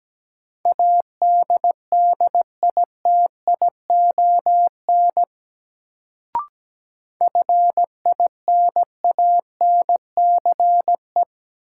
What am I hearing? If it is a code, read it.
Morse code, heard as a keyed tone, about 17 wpm, 700 Hz. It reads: ADDITION FINANCE
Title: Sentences from Top 1500 Words - New Word Drill Alt - 17wpm